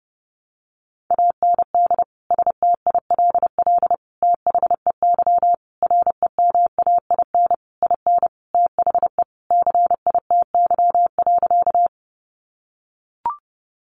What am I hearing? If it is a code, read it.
Morse code, heard as a keyed tone, about 30 wpm, 700 Hz. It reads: AND STILL THEY REMAIN IN THE CITY.
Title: Sentences from Top 1200 Words 30wpm